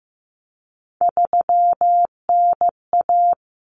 Morse code, heard as a keyed tone, about 15 wpm, 700 Hz. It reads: 3NA
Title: Common Cut Numbers - 3-Digit 15wpm